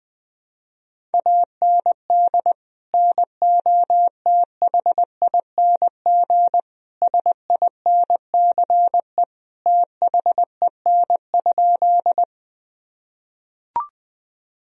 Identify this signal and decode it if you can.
Morse code, heard as a keyed tone, about 20 wpm, 700 Hz. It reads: AND NOTHING SINCE THEN?
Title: Sentences from Top 1000 Words - Review Part 1 - 20wpm